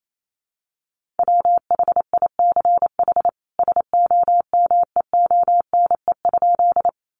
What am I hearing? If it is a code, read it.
Morse code, heard as a keyed tone, about 28 wpm, 700 Hz. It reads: WHICH SOMEONE?
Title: Sentences from Top 600 Words 28wpm